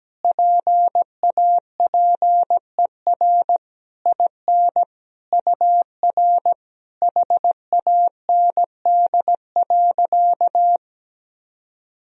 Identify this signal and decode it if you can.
Morse code, heard as a keyed tone, about 17 wpm, 700 Hz. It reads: PAPER IN UR HAND.
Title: Sentences from Top 400 Words and Abbreviations 17wpm